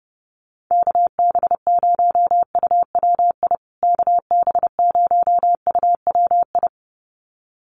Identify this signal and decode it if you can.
Morse code, heard as a keyed tone, about 30 wpm, 700 Hz. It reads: KB0UWI KB0UWI
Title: USA Call Signs Repeated 2x 30wpm